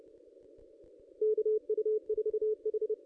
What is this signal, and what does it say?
Morse code, heard as a keyed tone, about 26 wpm, 440 Hz. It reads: KU4H
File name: ss-cw-k5tr-op-n5rz